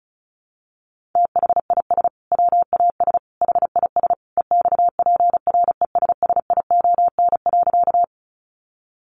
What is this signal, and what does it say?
Morse code, heard as a keyed tone, about 34 wpm, 700 Hz. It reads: THIS WAS HIS EXPRESSION.
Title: Sentences from Top 1200 Words 35wpm